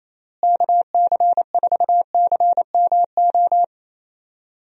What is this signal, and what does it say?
Morse code, heard as a keyed tone, about 28 wpm, 700 Hz. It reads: KC4CMO
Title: USA Call Signs - 6 Characters - Repeated 3x - 28wpm